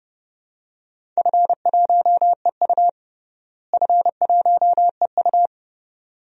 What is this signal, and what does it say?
Morse code, heard as a keyed tone, about 30 wpm, 700 Hz. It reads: F1EU F1EU
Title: DXCC-100 Call Signs - 4 Characters - Repeated 2x - 30wpm